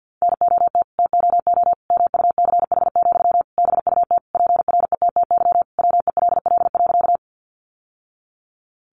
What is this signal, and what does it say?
Morse code, heard as a keyed tone, about 50 wpm, 700 Hz. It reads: NOT TOO MUCH, BUT PRETTY WELL.
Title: Sentences from Top 1000 Words 50wpm